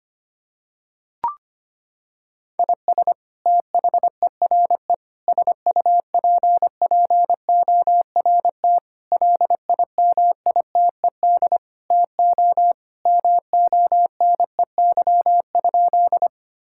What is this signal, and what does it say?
Morse code, heard as a keyed tone, about 25 wpm, 700 Hz. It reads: IS THERE SUPPORT LIMITED TO MONEY?
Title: Sentences from Top 1000 Words - Review Part 6 - 25wpm